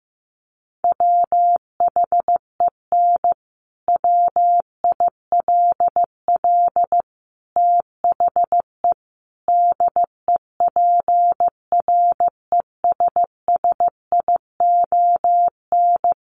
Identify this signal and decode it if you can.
Morse code, heard as a keyed tone, about 14 wpm, 700 Hz. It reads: WHEN WILL THE DEPRESSION
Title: Sentences from Top 1300 Words 15wpm